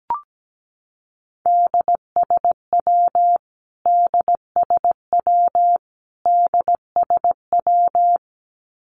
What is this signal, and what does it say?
Morse code, heard as a keyed tone, about 17 wpm, 700 Hz. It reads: DSW DSW DSW
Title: Abbreviations Repeated 3x - 17wpm